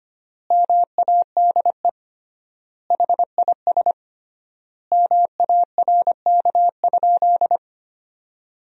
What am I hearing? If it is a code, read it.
Morse code, heard as a keyed tone, about 25 wpm, 700 Hz. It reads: MADE HIS MARK?
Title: 3x Word-Spacing - Sentences from Top 300 Words - 25wpm